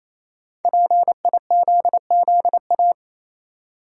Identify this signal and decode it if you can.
Morse code, heard as a keyed tone, about 28 wpm, 700 Hz. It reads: PIZZA
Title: Five Letter Words - 28wpm